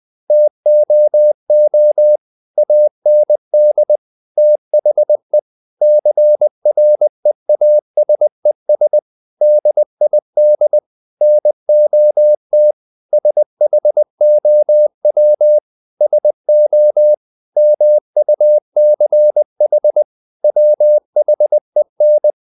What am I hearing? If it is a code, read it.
Morse code, heard as a keyed tone, about 20 wpm, 600 Hz. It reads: TOO AND THE CREASES DID NOT SHOW SO MUCH WHEN
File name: the old man and the sea / the old man and the sea part 1 20wpm